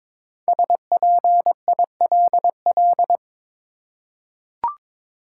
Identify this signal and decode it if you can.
Morse code, heard as a keyed tone, about 22 wpm, 700 Hz. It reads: SPILL
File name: 5 Letter Words - First 3 Letters Given - 22wpm